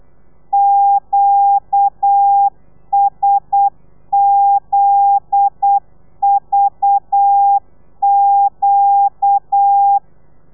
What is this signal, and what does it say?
Morse code, heard as a keyed tone, about 8 wpm, 800 Hz. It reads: QSZVQ
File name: telegrafia / i2tfj / Lez9 35